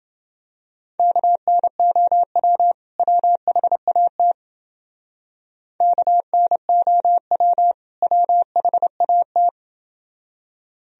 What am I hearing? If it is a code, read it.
Morse code, heard as a keyed tone, about 29 wpm, 700 Hz. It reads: KNOW WHAT KNOW WHAT